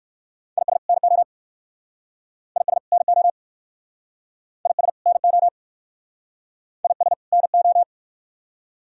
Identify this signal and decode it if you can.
Morse code, heard as a keyed tone, about 63 wpm, 700 Hz. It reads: IS NO IS NO IS NO IS NO